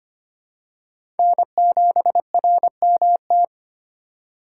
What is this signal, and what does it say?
Morse code, heard as a keyed tone, about 25 wpm, 700 Hz. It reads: N7RMT
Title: USA Call Signs - 5 Characters - 4x Character Spacing - 25wpm